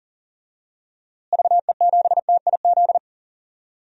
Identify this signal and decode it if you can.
Morse code, heard as a keyed tone, about 40 wpm, 700 Hz. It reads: VE7TIZ
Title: DXCC-100 Call Signs - 6 Characters - Repeated 2x - 40wpm